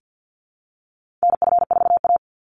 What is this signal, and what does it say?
Morse code, heard as a keyed tone, about 49 wpm, 700 Hz. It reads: NF4A
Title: Call Signs Repeated 3x - 50wpm